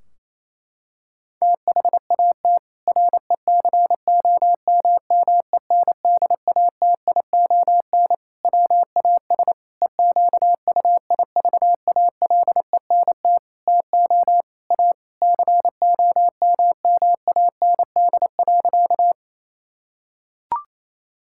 Morse code, heard as a keyed tone, about 28 wpm, 700 Hz. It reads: THAT RECOMMENDATION WAS EQUIVALENT TO A COMMAND.
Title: Sentences from Top 1600 Words 28wpm